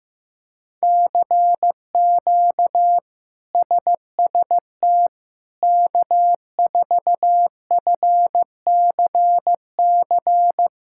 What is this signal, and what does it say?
Morse code, heard as a keyed tone, about 16 wpm, 700 Hz. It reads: CQ SST K4FCC
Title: K1USN SST QSO Practice - 15wpm